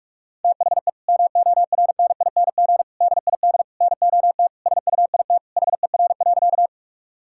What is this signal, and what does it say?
Morse code, heard as a keyed tone, about 43 wpm, 700 Hz. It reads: THE MORNING DID NOT SUIT HER.
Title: Sentences from Top 1600 Words 45wpm